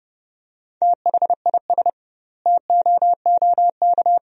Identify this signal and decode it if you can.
Morse code, heard as a keyed tone, about 30 wpm, 700 Hz. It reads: THIS TOOK
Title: 2x Word-Spacing - Sentences from Top 300 Words - 30wpm